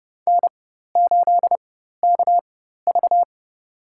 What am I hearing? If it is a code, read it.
Morse code, heard as a keyed tone, about 30 wpm, spaced as for 14 wpm, 690 Hz. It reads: N8KV
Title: USA Call Signs - 4x Character Spacing - 30wpm